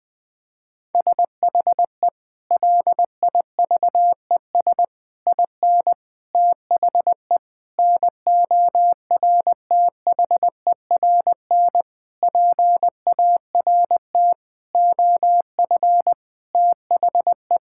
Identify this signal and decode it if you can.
Morse code, heard as a keyed tone, about 20 wpm, 700 Hz. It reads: SHE LIVES IN THE NORTHERN PART OF THE
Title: Sentences with 6-Word Phrases - 20wpm